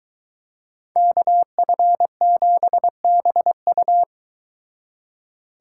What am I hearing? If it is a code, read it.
Morse code, heard as a keyed tone, about 22 wpm, 700 Hz. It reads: KF7BU